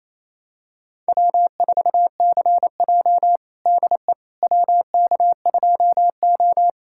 Morse code, heard as a keyed tone, about 28 wpm, 700 Hz. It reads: W4CJ DE WK2O
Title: USA Callsign DE Callsign - 4 Characters - 28wpm